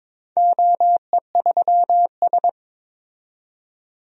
Morse code, heard as a keyed tone, about 22 wpm, 700 Hz. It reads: OE3S
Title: DXCC-100 Call Signs - 4 Characters - 22wpm